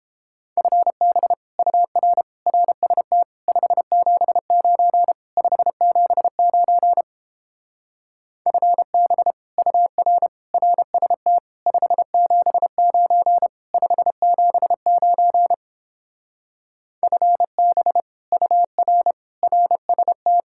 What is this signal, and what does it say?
Morse code, heard as a keyed tone, about 29 wpm, 700 Hz. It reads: FB UR RST 579 579 FB UR RST 579 579 FB UR RST